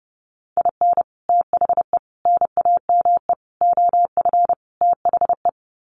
Morse code, heard as a keyed tone, about 30 wpm, 700 Hz. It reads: IN THE NAME OF THE
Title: Sets of 5 Words Encore 30wpm